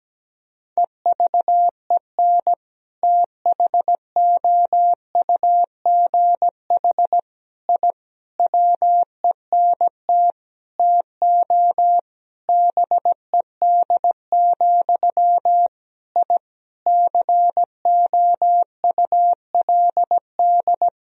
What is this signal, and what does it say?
Morse code, heard as a keyed tone, about 17 wpm, 700 Hz. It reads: EVEN THOUGH I WENT TO BED, I COULD
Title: Sentences from Top 1000 Words - Review Part 2 - 17wpm